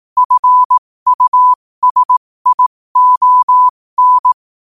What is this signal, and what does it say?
Morse code, heard as a keyed tone, about 18 wpm, spaced as for 15 wpm, 1000 Hz. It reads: FUSION
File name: qotdfiles / 15wpm